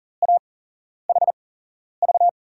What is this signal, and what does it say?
Morse code, heard as a keyed tone, about 40 wpm, 700 Hz. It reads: A H V
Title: USA Call Signs - 6 Characters - 8x Character Spacing - 40wpm